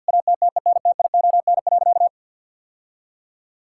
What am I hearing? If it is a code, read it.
Morse code, heard as a keyed tone, about 46 wpm, 700 Hz. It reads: ATTENTION.
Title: Sentences from Top 1000 Words - Review Part 6 - 50wpm